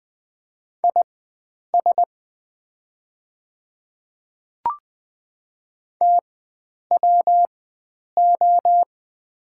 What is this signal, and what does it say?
Morse code, heard as a keyed tone, about 20 wpm, spaced as for 9 wpm, 700 Hz. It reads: IS TWO
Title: 4x Character Spacing - Top 100 Words - 20wpm